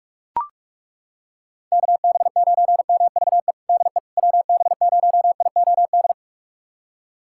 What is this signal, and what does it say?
Morse code, heard as a keyed tone, about 42 wpm, 690 Hz. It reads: KB9MVE DE WB0IOD
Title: USA Callsign DE Callsign 45wpm